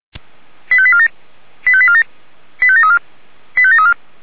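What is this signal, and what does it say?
Five-tone selective call (ZVEI), heard as quick runs of five tones, tones of about 70 ms. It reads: 85847 85847 85734 85734
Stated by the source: ZVEI five-tone selective call received on SDR.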